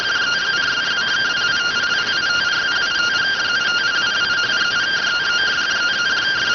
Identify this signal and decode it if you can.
Two-tone teleprinter signal, about 170 Hz shift.
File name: findings / axial iodine 3mm 205.gen-12 rx